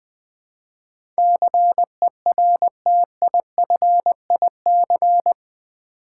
Morse code, heard as a keyed tone, about 21 wpm, 700 Hz. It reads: CERTIFIC
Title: Build A Word - 4 Letters At A Time - 11-Letter Words - 20wpm